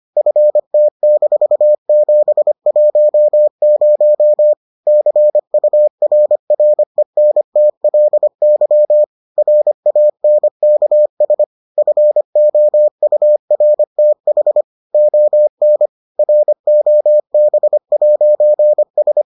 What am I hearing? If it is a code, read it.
Morse code, heard as a keyed tone, about 25 wpm, 600 Hz. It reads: FT-710 CURRENTLY RANKS FOURTH ON ROB'S